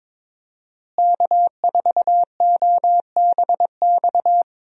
Morse code, heard as a keyed tone, about 22 wpm, 700 Hz. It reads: K4OBX
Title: USA Call Signs - 5 Characters - Repeated 2x - 22wpm